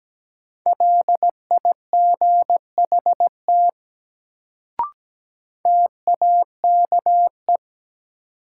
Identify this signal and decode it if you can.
Morse code, heard as a keyed tone, about 17 wpm, 700 Hz. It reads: LIGHT TAKE
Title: Sentences from Top 200 Words - New Word Drill - 17wpm